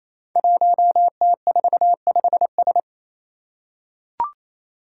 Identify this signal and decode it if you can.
Morse code, heard as a keyed tone, about 28 wpm, 700 Hz. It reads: 1T45S
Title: Five Challenging Letter-Number Combinations 28wpm